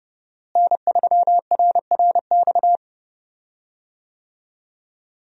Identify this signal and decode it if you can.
Morse code, heard as a keyed tone, about 30 wpm, 700 Hz. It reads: N3RRX
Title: USA Call Signs - 5 Characters - Repeated 2x - 30wpm